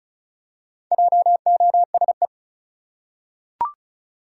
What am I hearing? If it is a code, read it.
Morse code, heard as a keyed tone, about 34 wpm, 700 Hz. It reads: JOSE
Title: USA Common Names - Max 5 Letters - 35wpm